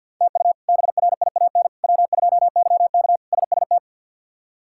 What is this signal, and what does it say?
Morse code, heard as a keyed tone, about 52 wpm, 700 Hz. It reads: TU BRIAN W2YK SST
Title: K1USN SST QSO Practice - 50wpm